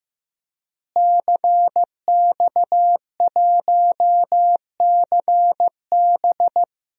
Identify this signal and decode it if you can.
Morse code, heard as a keyed tone, about 16 wpm, 700 Hz. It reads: CX1CB
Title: DXCC-100 Call Signs - 5 Characters - Repeated 3x - 15wpm